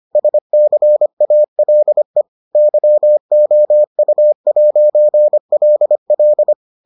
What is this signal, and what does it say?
Morse code, heard as a keyed tone, about 25 wpm, 600 Hz. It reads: SCALE YOU'LL